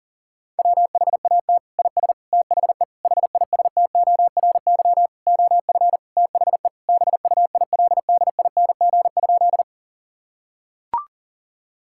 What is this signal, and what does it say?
Morse code, heard as a keyed tone, about 37 wpm, 700 Hz. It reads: WHAT IS THE HISTORY OF THE BUILDING?
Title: Sentences from Top 1000 Words - Review Part 5 - 40wpm